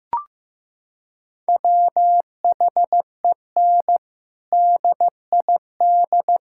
Morse code, heard as a keyed tone, about 16 wpm, 700 Hz. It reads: WHEN DID
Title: Sentences from Top 1000 Words - Review Part 6 - 15wpm